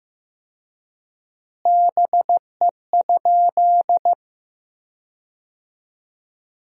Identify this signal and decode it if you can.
Morse code, heard as a keyed tone, about 16 wpm, 700 Hz. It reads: BE?
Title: 4x Word-Spacing - Sentences from Top 100 Words - 15wpm